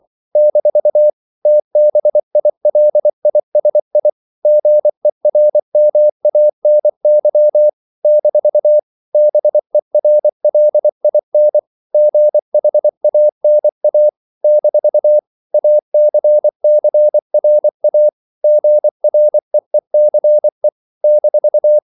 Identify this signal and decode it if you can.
Morse code, heard as a keyed tone, about 24 wpm, 600 Hz. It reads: - TBILISI GERMANY - BERLIN GHANA - ACCRA GREECE -
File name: CW / Practice / NationalCapitals24